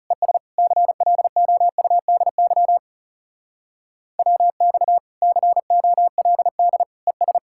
Morse code, heard as a keyed tone, about 37 wpm, 700 Hz. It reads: ES CLOUDY WX COLD ES
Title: Speed-Racing Common QSO Phrases - 30wpm